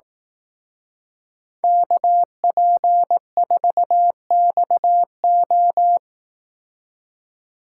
Morse code, heard as a keyed tone, about 17 wpm, 700 Hz. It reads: KP4XO
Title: Speed Racing - USA 2x2 Call Signs 15wpm